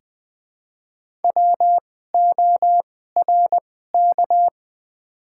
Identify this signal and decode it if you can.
Morse code, heard as a keyed tone, about 20 wpm, spaced as for 15 wpm, 700 Hz. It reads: WORK
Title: 2x Character Spacing - Top 300 Words - 20wpm